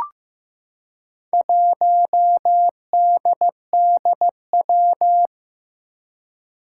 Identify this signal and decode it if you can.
Morse code, heard as a keyed tone, about 15 wpm, 700 Hz. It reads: 1DDW
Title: Four Challenging Letter-Number Combinations 15wpm